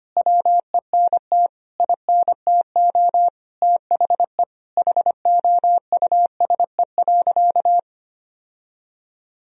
Morse code, heard as a keyed tone, about 25 wpm, 700 Hz. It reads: WENT INTO THE HOUSE.
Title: Sentences from Top 1000 Words - Review Part 1 - 25wpm